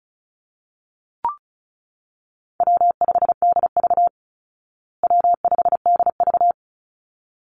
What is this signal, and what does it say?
Morse code, heard as a keyed tone, about 34 wpm, 700 Hz. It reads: W5DV W5DV
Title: Call Signs Repeated 3x - 35wpm